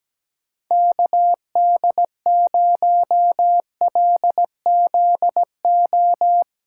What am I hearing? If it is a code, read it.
Morse code, heard as a keyed tone, about 17 wpm, 700 Hz. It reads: KD0LZO
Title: USA Callsign DE Callsign - 6 Characters - 17wpm